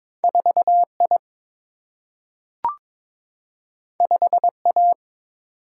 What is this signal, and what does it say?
Morse code, heard as a keyed tone, about 21 wpm, 700 Hz. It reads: 4I 5A
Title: Single-Letter Single-Number 22wpm